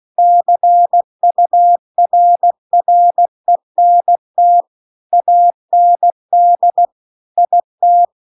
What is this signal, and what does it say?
Morse code, heard as a keyed tone, about 16 wpm, 700 Hz. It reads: CURRENT AND IT